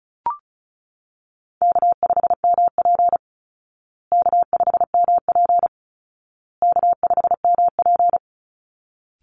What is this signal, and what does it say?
Morse code, heard as a keyed tone, about 34 wpm, 700 Hz. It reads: K5MP K5MP K5MP
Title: Easy Call Signs Repeated 3x - 35wpm